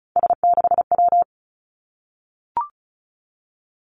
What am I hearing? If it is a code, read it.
Morse code, heard as a keyed tone, about 36 wpm, 700 Hz. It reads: S6W
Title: Three Challenging Letter-Number Combinations 35wpm